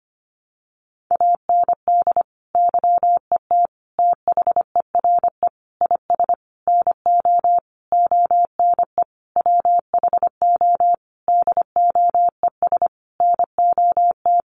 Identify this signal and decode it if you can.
Morse code, heard as a keyed tone, about 25 wpm, 700 Hz. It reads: AND YET THERE IS NO ONE WHO DOES NOT
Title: Sentences from Top 1000 Words 25wpm